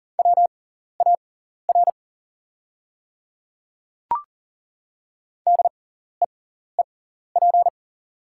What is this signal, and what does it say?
Morse code, heard as a keyed tone, about 40 wpm, spaced as for 14 wpm, 690 Hz. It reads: WAR DEEP
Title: 6x Character Spacing - Top 500 Words - 40wpm